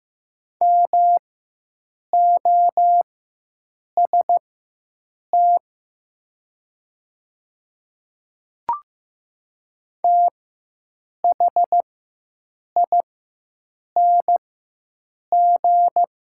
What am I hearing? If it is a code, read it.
Morse code, heard as a keyed tone, about 15 wpm, spaced as for 7 wpm, 700 Hz. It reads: MOST THING
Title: 4x Character Spacing - Top 100 Words - 15wpm